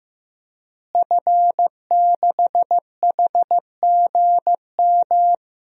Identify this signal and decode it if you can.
Morse code, heard as a keyed tone, about 15 wpm, 700 Hz. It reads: F6HGM
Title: DXCC-100 Call Signs - 5 Characters - Repeated 2x - 15wpm